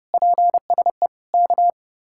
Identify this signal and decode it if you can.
Morse code, heard as a keyed tone, about 40 wpm, 700 Hz. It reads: PSE K